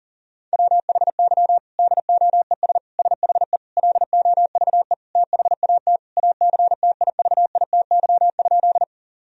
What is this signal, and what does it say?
Morse code, heard as a keyed tone, about 40 wpm, 700 Hz. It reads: WHY DOES SHE LOVE THAT ACTIVITY?
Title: Sentences from Top 1000 Words - Review Part 4 - 40wpm